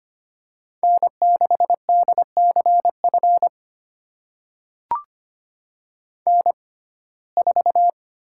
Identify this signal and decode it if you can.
Morse code, heard as a keyed tone, about 25 wpm, spaced as for 18 wpm, 700 Hz. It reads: N6DCF N4
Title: USA Call Signs - 5 Characters - 6x Character Spacing - 25wpm